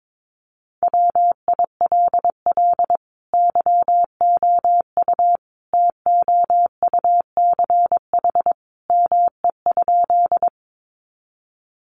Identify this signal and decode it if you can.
Morse code, heard as a keyed tone, about 22 wpm, 700 Hz. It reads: WILL YOU TOUCH ME?
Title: Sentences from Top 1200 Words 22wpm